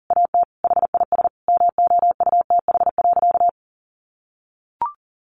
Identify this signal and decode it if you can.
Morse code, heard as a keyed tone, about 40 wpm, 700 Hz. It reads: AT HIS MOUTH.
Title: Sentences from Top 1500 Words 40wpm